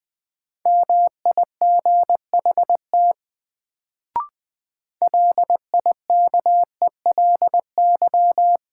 Morse code, heard as a keyed tone, about 20 wpm, 700 Hz. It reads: MIGHT LIKELY
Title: Sets of 2 Words - New Word Drill - 20wpm